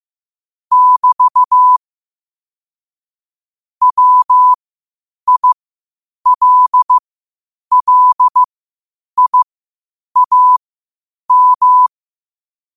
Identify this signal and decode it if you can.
Morse code, heard as a keyed tone, about 15 wpm, spaced as for 8 wpm, 1000 Hz. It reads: <BT> WILLIAM
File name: qotdfiles / 8wpm